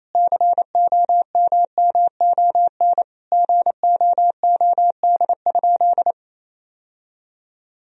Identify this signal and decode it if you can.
Morse code, heard as a keyed tone, about 28 wpm, 700 Hz. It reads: COMMON GOOD?
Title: Sentences from Top 1000 Words - Review Part 3 - 28wpm